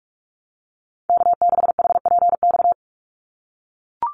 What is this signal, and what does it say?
Morse code, heard as a keyed tone, about 44 wpm, 700 Hz. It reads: K6HPX
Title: NA SOTA Chaser Callsigns 45wpm